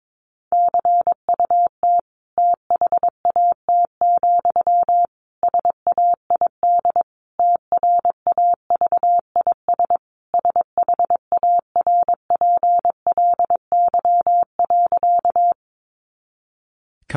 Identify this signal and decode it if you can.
Morse code, heard as a keyed tone, about 22 wpm, 700 Hz. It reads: CUT THAT, SAID TRAVIS SHARPLY.
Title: A Sound Of Thunder 22wpm